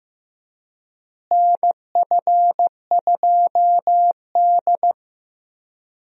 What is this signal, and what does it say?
Morse code, heard as a keyed tone, about 15 wpm, 700 Hz. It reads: NF2D
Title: USA Call Signs - 4 Characters - Repeated 3x - 15wpm